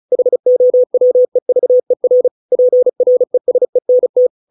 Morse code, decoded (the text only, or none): HOWEVER PRESENT